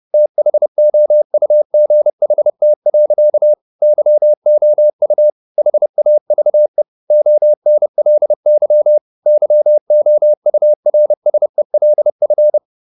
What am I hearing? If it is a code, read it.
Morse code, heard as a keyed tone, about 30 wpm, 600 Hz. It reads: THOUGHT. YOU HAVE ONLY YOURSELF
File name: the old man and the sea / the old man and the sea 1 30wpm